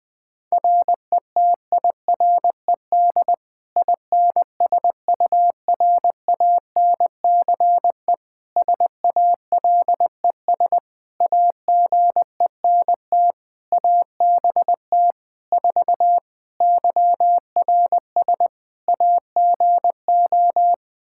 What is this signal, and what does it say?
Morse code, heard as a keyed tone, about 20 wpm, 700 Hz. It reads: RETIRED INSURANCE SALES AGENT ABT 4 YRS AGO
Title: K1USN SST QSO Practice 20wpm